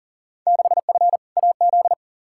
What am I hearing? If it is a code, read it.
Morse code, heard as a keyed tone, about 40 wpm, 700 Hz. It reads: 6F AZ
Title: Field Day Exchanges Encore - 40wpm